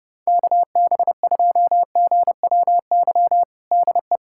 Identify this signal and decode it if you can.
Morse code, heard as a keyed tone, about 32 wpm, 700 Hz. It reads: KB2GWY DE
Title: USA Callsign DE Callsign - 6 Characters - 30wpm